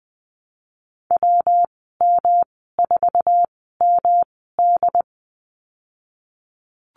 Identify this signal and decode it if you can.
Morse code, heard as a keyed tone, about 20 wpm, spaced as for 15 wpm, 700 Hz. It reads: WM4MD
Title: 128 Lesson 21g - Number 6 - Call Signs 20wpm